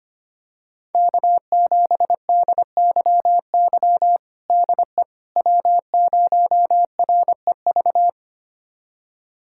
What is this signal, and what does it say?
Morse code, heard as a keyed tone, about 25 wpm, 700 Hz. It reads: K7DYY DE W0REV
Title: USA Callsign DE Callsign - 5 Characters - 25wpm